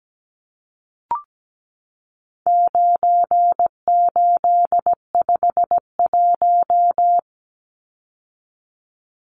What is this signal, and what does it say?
Morse code, heard as a keyed tone, about 17 wpm, 700 Hz. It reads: 9851
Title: Numbers - 4-Digit 17wpm